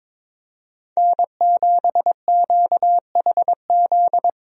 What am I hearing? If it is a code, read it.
Morse code, heard as a keyed tone, about 22 wpm, 700 Hz. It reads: N7QHZ
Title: USA Call Signs - 5 Characters - Repeated 2x - 22wpm